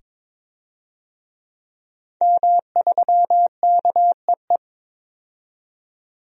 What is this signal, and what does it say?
Morse code, heard as a keyed tone, about 22 wpm, 700 Hz. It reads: M3KEE
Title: DXCC-100 Call Signs - 5 Characters - 22wpm